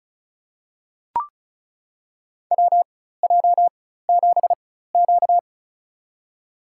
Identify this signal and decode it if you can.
Morse code, heard as a keyed tone, about 36 wpm, spaced as for 16 wpm, 700 Hz. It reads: WJ7Q
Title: USA Call Signs - 4 Characters - 4x Character Spacing - 35wpm